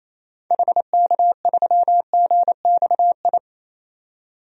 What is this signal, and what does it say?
Morse code, heard as a keyed tone, about 27 wpm, 700 Hz. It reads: HK3GXI
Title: DXCC-100 Call Signs - 6 Characters - 28wpm